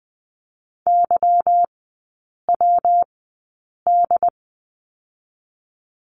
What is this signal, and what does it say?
Morse code, heard as a keyed tone, about 20 wpm, 700 Hz. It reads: Y W D
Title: 145 Lesson 24e - Letter y - Three Characters 20wpm